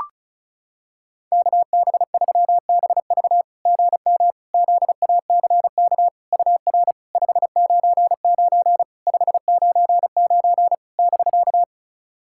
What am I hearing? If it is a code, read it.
Morse code, heard as a keyed tone, about 36 wpm, 690 Hz. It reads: KB3BV GM ZACK UR 599 599 <BK>
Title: SOTA QSO Practice 35wpm